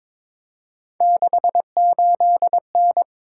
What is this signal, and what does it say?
Morse code, heard as a keyed tone, about 22 wpm, 700 Hz. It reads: 68N
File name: practice / 22wpm / Three Letter-Number Combinations - 22wpm